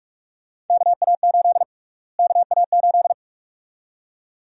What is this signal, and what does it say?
Morse code, heard as a keyed tone, about 46 wpm, 700 Hz. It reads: KA8 KA8
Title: USA Call Sign Prefixes - Repeated 2x - 45wpm